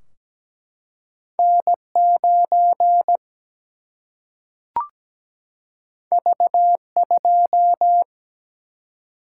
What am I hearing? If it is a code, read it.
Morse code, heard as a keyed tone, about 17 wpm, 700 Hz. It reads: N9 V2
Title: Single-Letter Single-Number 17wpm